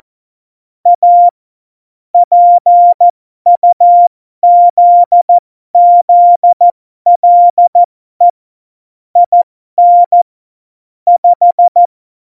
Morse code, heard as a keyed tone, about 14 wpm, spaced as for 12 wpm, 700 Hz. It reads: A PUZZLE IN 5
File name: Morsekurs / Witze